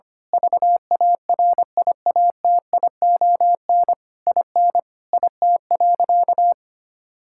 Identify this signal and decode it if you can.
Morse code, heard as a keyed tone, about 25 wpm, 700 Hz. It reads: VARIATION IN IT.
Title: Sentences from Top 1600 Words 25wpm